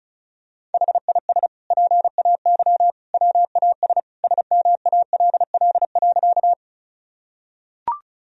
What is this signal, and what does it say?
Morse code, heard as a keyed tone, about 34 wpm, 700 Hz. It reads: HIS PAY WAS SMALL.
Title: Sentences from Top 1000 Words - Review Part 1 - 35wpm